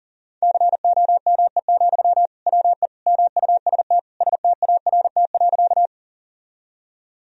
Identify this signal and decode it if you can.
Morse code, heard as a keyed tone, about 40 wpm, 700 Hz. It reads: COME, WE MUST START.
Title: Sentences from Top 1000 Words - Review Part 1 - 40wpm